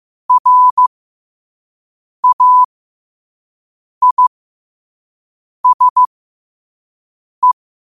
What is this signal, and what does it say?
Morse code, heard as a keyed tone, about 15 wpm, spaced as for 5 wpm, 1000 Hz. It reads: RAISE
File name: qotdfiles / 5wpm